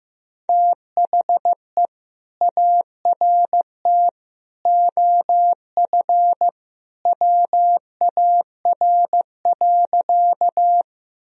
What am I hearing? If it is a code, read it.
Morse code, heard as a keyed tone, about 15 wpm, 700 Hz. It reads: THE ART OF WAR.
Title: Sentences from Top 1000 Words - Review Part 3 - 15wpm